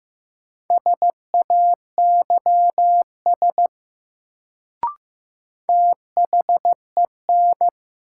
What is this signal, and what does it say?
Morse code, heard as a keyed tone, about 15 wpm, 700 Hz. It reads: SAYS THEN
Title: Sets of 2 Words - New Word Drill - 15wpm